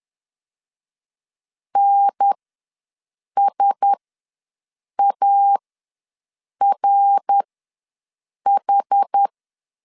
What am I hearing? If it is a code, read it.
Morse code, heard as a keyed tone, about 10 wpm, spaced as for 6 wpm, 800 Hz. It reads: NSARH
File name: LEZIONI / 5 LBVC / 5 LB ws 10wpm 3616